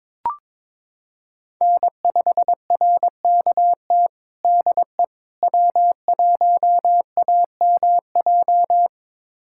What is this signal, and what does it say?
Morse code, heard as a keyed tone, about 23 wpm, 700 Hz. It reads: N5RKT DE W1AMJ
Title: USA Callsign DE Callsign - 5 Characters - 22wpm